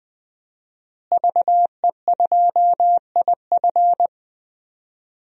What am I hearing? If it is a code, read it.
Morse code, heard as a keyed tone, about 20 wpm, 700 Hz. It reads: VE2IF
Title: DXCC-100 Call Signs - 5 Characters - Repeated 3x - 20wpm